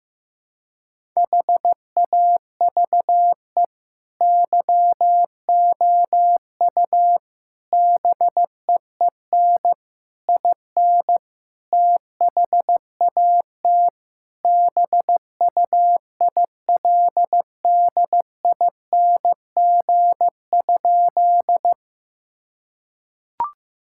Morse code, heard as a keyed tone, about 15 wpm, 700 Hz. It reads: HAVE YOU BEEN IN THAT BUILDING?
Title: Sentences from Top 1000 Words - Review Part 5 - 15wpm